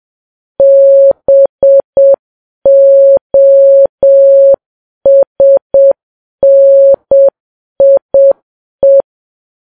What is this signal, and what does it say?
Morse code, heard as a keyed tone, about 7 wpm, 550 Hz. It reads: BOSNIE
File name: Files Web tls / Morse on9fv / morse7kps